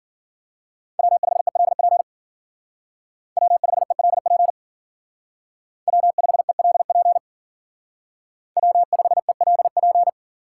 Morse code, heard as a keyed tone, about 52 wpm, 700 Hz. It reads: W5ELP W5ELP W5ELP W5ELP